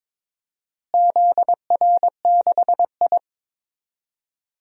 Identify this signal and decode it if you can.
Morse code, heard as a keyed tone, about 22 wpm, 700 Hz. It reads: ZR6I
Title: DXCC-100 Call Signs - 4 Characters - Repeated 3x - 22wpm